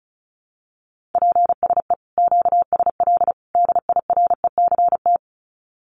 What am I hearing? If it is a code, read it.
Morse code, heard as a keyed tone, about 34 wpm, 700 Hz. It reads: PSE QSL DIRECT
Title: Common QSO Phrases 35wpm